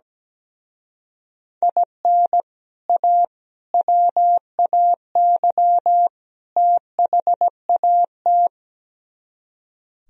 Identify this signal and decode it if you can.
Morse code, heard as a keyed tone, about 17 wpm, 700 Hz. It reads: IN A WAY THAT
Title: Speed-Racing - Sets of 4 Words 17wpm